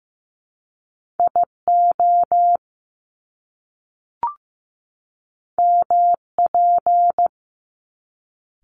Morse code, heard as a keyed tone, about 15 wpm, 700 Hz. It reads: IO MP
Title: Top Two Letter Combinations in Words 15wpm